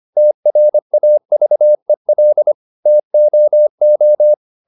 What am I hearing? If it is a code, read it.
Morse code, heard as a keyed tone, about 25 wpm, 600 Hz. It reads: TRAVEL TOO
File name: the old man and the sea part 1 25wpm